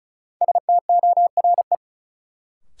Morse code, heard as a keyed tone, about 34 wpm, 700 Hz. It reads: STORE
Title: Sentences from Top 1000 Words - Review Part 5 - New Word Drill - 35wpm